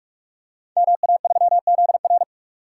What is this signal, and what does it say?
Morse code, heard as a keyed tone, about 44 wpm, 700 Hz. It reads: MA3ZR
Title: Code Groups - 5 Letter:Numbers - 45wpm